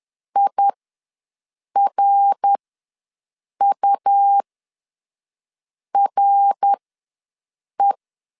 Morse code, heard as a keyed tone, about 10 wpm, spaced as for 5 wpm, 800 Hz. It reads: IRURE
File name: LEZIONI / 5 LBVC / 5 LBV ws 10wpm 3645